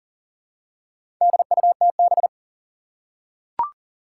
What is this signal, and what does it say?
Morse code, heard as a keyed tone, about 40 wpm, 690 Hz. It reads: DUTB
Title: Four Challenging Letter-Number Combinations 40wpm